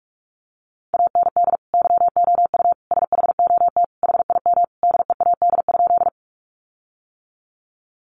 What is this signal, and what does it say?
Morse code, heard as a keyed tone, about 44 wpm, 700 Hz. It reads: AND YOU SHOT HIM DEAD?
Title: Sentences from Top 1500 Words 45wpm